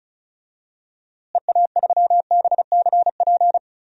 Morse code, heard as a keyed tone, about 36 wpm, 700 Hz. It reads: EA3BCP
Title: DXCC-100 Call Signs - 6 Characters - Repeated 3x - 35wpm